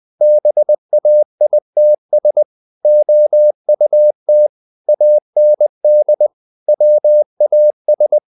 Morse code, heard as a keyed tone, about 20 wpm, 600 Hz. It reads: BAITS OUT AND WAS
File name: the old man and the sea / the old man and the sea part 1 20wpm